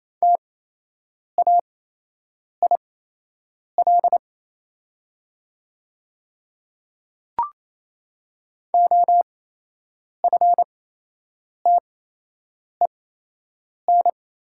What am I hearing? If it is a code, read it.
Morse code, heard as a keyed tone, about 28 wpm, spaced as for 8 wpm, 700 Hz. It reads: TAIL OFTEN
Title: 8x Character Spacing - Top 500 Words - 28wpm